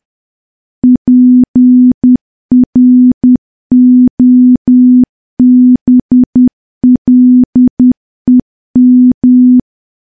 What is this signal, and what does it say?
Morse code, heard as a keyed tone, about 10 wpm, 250 Hz. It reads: PROBLEM
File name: morse / morse-training-10WPM